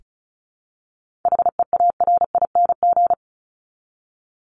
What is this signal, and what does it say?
Morse code, heard as a keyed tone, about 34 wpm, 700 Hz. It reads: HEARING
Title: Top 1751-2000 Words 35wpm